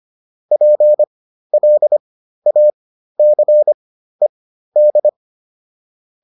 Morse code, heard as a keyed tone, about 25 wpm, spaced as for 13 wpm, 600 Hz. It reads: PLACED